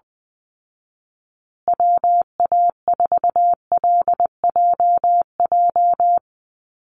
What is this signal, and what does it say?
Morse code, heard as a keyed tone, about 19 wpm, 700 Hz. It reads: WA4LJJ
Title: Call Signs Repeated 3x - 20wpm